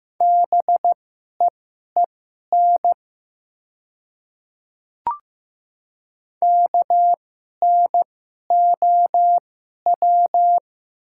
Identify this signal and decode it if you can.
Morse code, heard as a keyed tone, about 15 wpm, spaced as for 11 wpm, 700 Hz. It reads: BEEN KNOW
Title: 2x Character Spacing - Top 100 Words - 15wpm